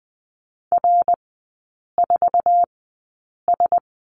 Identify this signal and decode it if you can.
Morse code, heard as a keyed tone, about 20 wpm, spaced as for 8 wpm, 700 Hz. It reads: R4S
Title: 074 Lesson 13e - Letter L - Three Characters 20wpm